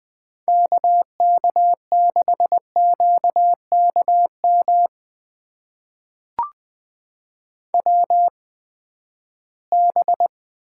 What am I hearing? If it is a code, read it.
Morse code, heard as a keyed tone, about 20 wpm, spaced as for 14 wpm, 700 Hz. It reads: KK6QKM WB
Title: USA Call Signs - 8x Character Spacing - 20wpm